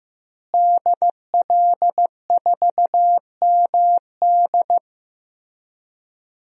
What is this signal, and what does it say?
Morse code, heard as a keyed tone, about 15 wpm, 700 Hz. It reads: DL4MD
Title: DXCC-100 Call Signs - 5 Characters - Repeated 3x - 15wpm